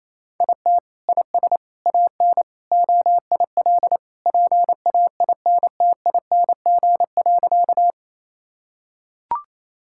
Morse code, heard as a keyed tone, about 28 wpm, 700 Hz. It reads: IT IS AN OIL PAINTING.
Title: Sentences from Top 1000 Words - Review Part 6 - 28wpm